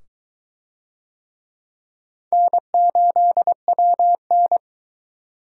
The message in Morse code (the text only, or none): N8WN